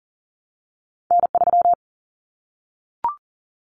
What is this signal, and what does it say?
Morse code, heard as a keyed tone, about 40 wpm, 700 Hz. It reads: N3
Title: Callsign Prefixes for Top 25 DXCC - 40wpm